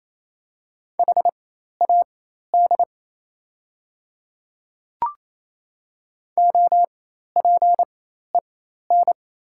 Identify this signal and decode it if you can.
Morse code, heard as a keyed tone, about 28 wpm, spaced as for 13 wpm, 700 Hz. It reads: HAD OPEN
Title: 4x Character Spacing - Top 300 Words - 28wpm